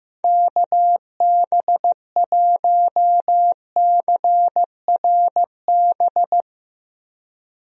Morse code, heard as a keyed tone, about 15 wpm, 700 Hz. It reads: KB1CRB
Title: USA Call Signs - 6 Characters - Repeated 2x - 15wpm